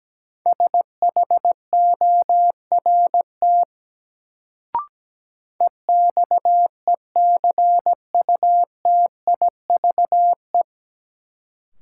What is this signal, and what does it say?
Morse code, heard as a keyed tone, about 17 wpm, 700 Hz. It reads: SHORT EXECUTIVE
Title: Sets of 5 Words Encore - New Word Drill - 17wpm